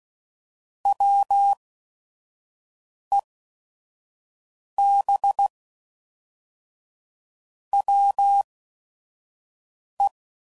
Morse code, heard as a keyed tone, about 16 wpm, spaced as for 5 wpm, 800 Hz. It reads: WEB WE
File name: Sayings / Quotes / Deceive